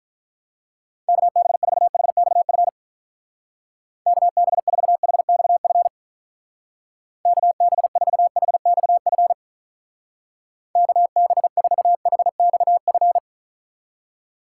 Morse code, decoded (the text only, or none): KB4HXF KB4HXF KB4HXF KB4HXF